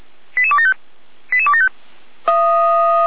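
Five-tone selective call (ZVEI), heard as quick runs of five tones, tones of about 70 ms. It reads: 88275 88275
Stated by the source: SDR-received ZVEI five-tone call.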